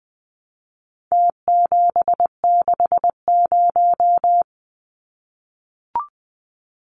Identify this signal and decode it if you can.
Morse code, heard as a keyed tone, about 20 wpm, 700 Hz. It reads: T760
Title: Numbers - Contest Style - 4-Digit 20wpm